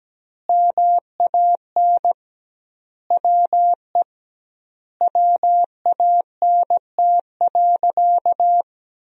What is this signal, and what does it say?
Morse code, heard as a keyed tone, about 17 wpm, 700 Hz. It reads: MAN WE WANT.
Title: 2x Word-Spacing - Sentences from Top 200 Words - 17wpm